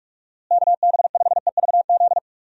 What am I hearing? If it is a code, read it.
Morse code, heard as a keyed tone, about 46 wpm, 700 Hz. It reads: KB5EVZ
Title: USA Call Signs - 6 Characters - Repeated 2x - 45wpm